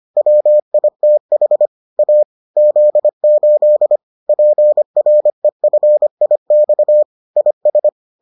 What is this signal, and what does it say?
Morse code, heard as a keyed tone, about 25 wpm, 600 Hz. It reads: WITH A Z8 PREFIX IS